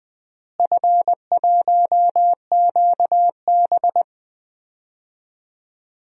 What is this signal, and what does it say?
Morse code, heard as a keyed tone, about 20 wpm, 700 Hz. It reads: F1QB
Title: DXCC-100 Call Signs - 4 Characters - Repeated 2x - 20wpm